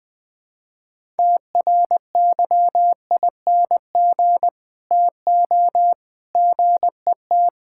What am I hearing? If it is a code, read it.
Morse code, heard as a keyed tone, about 20 wpm, 700 Hz. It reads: TRYING TO GET
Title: Build A Phrase - 3-Word Phrases - 20wpm